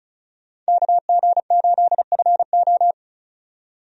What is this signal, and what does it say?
Morse code, heard as a keyed tone, about 36 wpm, 700 Hz. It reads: KG8FO
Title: USA Call Signs - 5 Characters - 35wpm